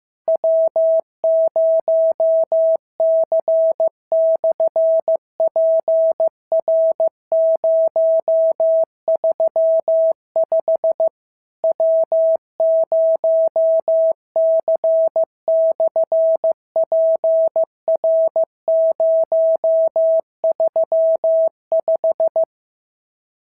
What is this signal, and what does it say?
Morse code, heard as a keyed tone, about 15 wpm, 650 Hz. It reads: W0C/PR035 W0C/PR035
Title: SOTA QSO Practice 15wpm